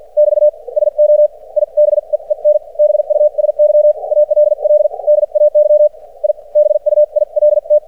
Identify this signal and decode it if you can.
Morse code, heard as a keyed tone, about 53 wpm, spaced as for 43 wpm, 600 Hz. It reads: <BT> SO I BEEN BUSY ALL DAY I BUILT
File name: archived / CRSnet